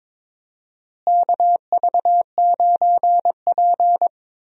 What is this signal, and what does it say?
Morse code, heard as a keyed tone, about 22 wpm, 700 Hz. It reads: KV9P
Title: USA Call Signs - 4 Characters - Repeated 3x - 22wpm